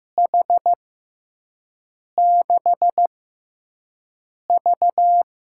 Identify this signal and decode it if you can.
Morse code, heard as a keyed tone, about 15 wpm, spaced as for 5 wpm, 700 Hz. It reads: H6V
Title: USA Call Signs - 6x Character Spacing - 15wpm